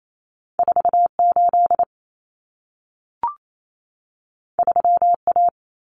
Morse code, heard as a keyed tone, about 28 wpm, 700 Hz. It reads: 48 3A
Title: Common Cut Numbers - 2-Digit 28wpm